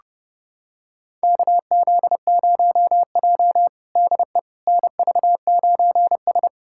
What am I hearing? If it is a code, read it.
Morse code, heard as a keyed tone, about 30 wpm, 700 Hz. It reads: KZ0J DE NV9S
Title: USA Callsign DE Callsign - 4 Characters - 30wpm